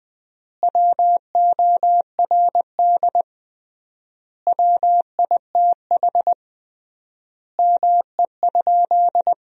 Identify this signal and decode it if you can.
Morse code, heard as a keyed tone, about 20 wpm, 700 Hz. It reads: WORD WITH ME?
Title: 3x Word-Spacing - Sentences from Top 300 Words - 20wpm